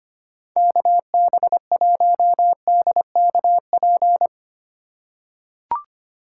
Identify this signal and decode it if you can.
Morse code, heard as a keyed tone, about 25 wpm, 690 Hz. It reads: KB1DKP
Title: USA Call Signs - 6 Characters - 2x Character Spacing - 25wpm